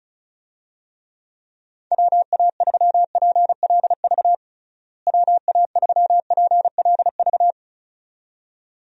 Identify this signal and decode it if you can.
Morse code, heard as a keyed tone, about 34 wpm, 700 Hz. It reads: WA3PLV WA3PLV
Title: USA Call Signs - 6 Characters - Repeated 2x - 35wpm